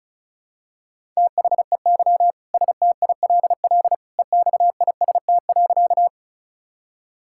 Morse code, heard as a keyed tone, about 34 wpm, 700 Hz. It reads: THEY STILL EXIST.
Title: Sentences from Top 1000 Words - Review Part 5 - 35wpm